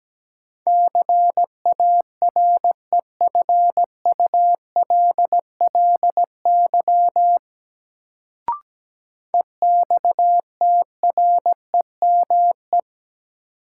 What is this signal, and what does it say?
Morse code, heard as a keyed tone, about 17 wpm, 700 Hz. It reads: CAREFULLY EXTREME
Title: Sentences from Top 1300 Words - New Word Drill Alt - 17wpm